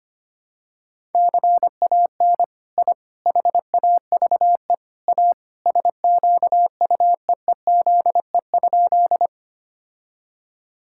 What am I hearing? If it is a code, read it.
Morse code, heard as a keyed tone, about 25 wpm, 700 Hz. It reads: CAN I HAVE A SQUEEZE?
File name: practice / 25wpm / All-Age Appropriate Jokes - 25wpm